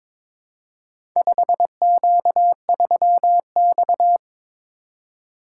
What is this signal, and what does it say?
Morse code, heard as a keyed tone, about 22 wpm, 700 Hz. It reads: 5Q3X
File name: practice / 22wpm / Four Letter-Number Combinations - 22wpm